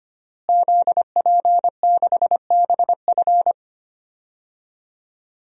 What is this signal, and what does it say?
Morse code, heard as a keyed tone, about 25 wpm, 700 Hz. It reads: ZP6BF
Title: DXCC-100 Call Signs - 5 Characters - Repeated 3x - 25wpm